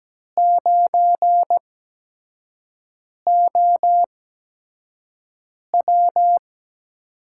Morse code, heard as a keyed tone, about 17 wpm, spaced as for 5 wpm, 700 Hz. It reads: 9OW
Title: USA Call Signs - 5 Characters - 8x Character Spacing - 17wpm